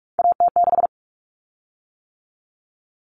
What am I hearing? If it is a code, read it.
Morse code, heard as a keyed tone, about 44 wpm, 700 Hz. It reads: AT6
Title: Common Cut Numbers - 3-Digit 45wpm